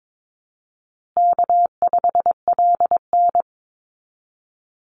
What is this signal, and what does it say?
Morse code, heard as a keyed tone, about 22 wpm, 700 Hz. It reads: K5LN
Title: Easy Call Signs Repeated 3x - 22wpm